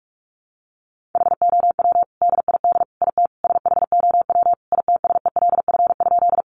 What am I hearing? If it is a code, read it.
Morse code, heard as a keyed tone, about 44 wpm, 700 Hz. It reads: HOW DID IT SHOW ITSELF?
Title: Sentences from Top 800 Words 45wpm